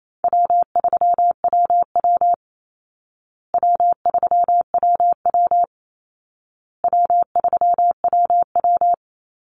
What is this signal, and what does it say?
Morse code, heard as a keyed tone, about 28 wpm, 700 Hz. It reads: W3WW W3WW W3WW
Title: Call Signs Repeated 2x - 28wpm